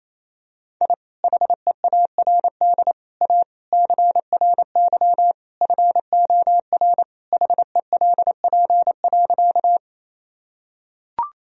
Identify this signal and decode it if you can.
Morse code, heard as a keyed tone, about 28 wpm, 700 Hz. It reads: I HEARD A CRY FOR HELP.
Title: Sentences from Top 1000 Words - Review Part 2 - 28wpm